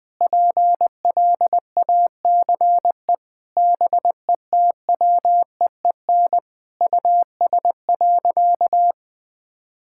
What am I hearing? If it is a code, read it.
Morse code, heard as a keyed tone, about 20 wpm, 700 Hz. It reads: PLACE BETWEEN US.
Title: Sentences from Top 1000 Words - Review Part 6 - 20wpm